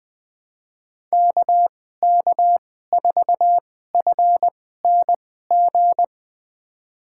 Morse code, heard as a keyed tone, about 20 wpm, spaced as for 14 wpm, 700 Hz. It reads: KK4FNG
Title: USA Call Signs - 6 Characters - 2x Character Spacing - 20wpm